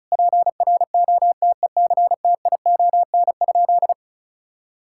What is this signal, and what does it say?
Morse code, heard as a keyed tone, about 34 wpm, 700 Hz. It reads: PROTECTION?
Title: Sentences from Top 1600 Words 35wpm